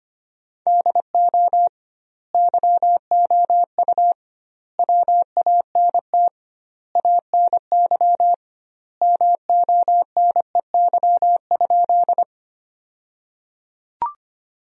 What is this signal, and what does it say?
Morse code, heard as a keyed tone, about 25 wpm, 700 Hz. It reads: DO YOU WANT ANY MONEY?
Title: 2x Word-Spacing - Sentences from Top 400 Words - 25wpm